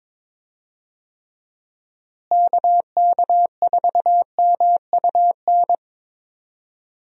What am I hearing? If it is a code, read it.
Morse code, heard as a keyed tone, about 22 wpm, 700 Hz. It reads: KK4MUN
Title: USA Call Signs - 6 Characters - Repeated 2x - 22wpm